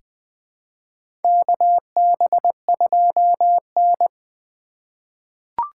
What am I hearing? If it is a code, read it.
Morse code, heard as a keyed tone, about 20 wpm, 700 Hz. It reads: KB2N
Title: USA Call Signs - 4 Characters - 2x Character Spacing - 20wpm